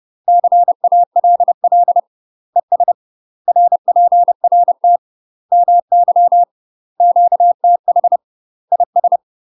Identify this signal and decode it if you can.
Morse code, heard as a keyed tone, about 30 wpm, 700 Hz. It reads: CALL ES RPRT MY QTH IS
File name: UK MASTER QSOs TRAINING / 30WPM / UK MASTER QSOs 0004